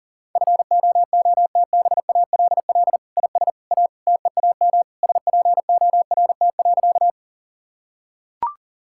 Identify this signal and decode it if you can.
Morse code, heard as a keyed tone, about 40 wpm, 700 Hz. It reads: FOOTBALL IS A TEAM SPORT.
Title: Sentences from Top 1600 Words 40wpm